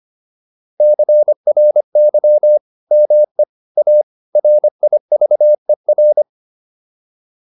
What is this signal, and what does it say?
Morse code, heard as a keyed tone, about 25 wpm, 600 Hz. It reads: CRY ME A RIVER